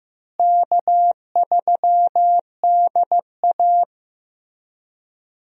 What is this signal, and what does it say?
Morse code, heard as a keyed tone, about 15 wpm, 700 Hz. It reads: K3DA
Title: USA Call Signs - 4 Characters - Repeated 3x - 15wpm